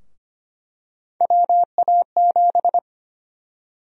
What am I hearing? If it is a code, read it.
Morse code, heard as a keyed tone, about 25 wpm, 700 Hz. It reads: WA7
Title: USA Call Sign Prefixes - 25wpm